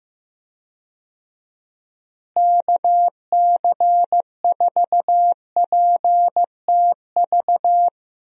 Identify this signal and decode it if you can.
Morse code, heard as a keyed tone, about 15 wpm, 700 Hz. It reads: KC4PTV
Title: USA Call Signs - 6 Characters - Repeated 2x - 15wpm